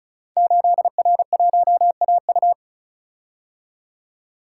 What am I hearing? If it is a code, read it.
Morse code, heard as a keyed tone, about 36 wpm, 690 Hz. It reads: 8R1AU
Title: Code Groups - 5 Letter:Numbers - 35wpm